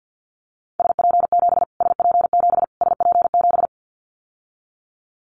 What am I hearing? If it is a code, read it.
Morse code, heard as a keyed tone, about 50 wpm, 700 Hz. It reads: SP7 SP7 SP7
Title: Callsign Prefixes for Top 25 DXCC - Repeated 3x - 50wpm